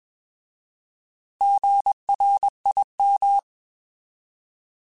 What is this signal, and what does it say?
Morse code, heard as a keyed tone, about 21 wpm, 800 Hz. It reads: GRIM
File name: Alphabetical / Alphabetical mp3s / Gs @20